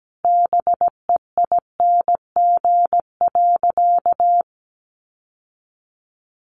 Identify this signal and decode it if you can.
Morse code, heard as a keyed tone, about 17 wpm, 700 Hz. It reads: BEING.
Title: Sentences from Top 1000 Words 17wpm